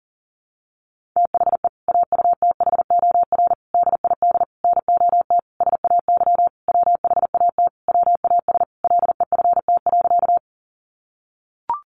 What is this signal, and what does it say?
Morse code, heard as a keyed tone, about 40 wpm, 700 Hz. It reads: THE AUTHOR DID NOT SAY WHAT WAS LEFT.
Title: Sentences from Top 1200 Words 40wpm